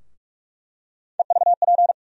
Morse code, heard as a keyed tone, about 46 wpm, 700 Hz. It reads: EVP
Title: USA Call Sign Postfixes - 3 Letters 45wpm